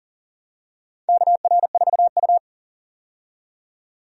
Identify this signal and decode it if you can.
Morse code, heard as a keyed tone, about 40 wpm, 700 Hz. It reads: KR4U
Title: USA Call Signs - 4 Characters - 40wpm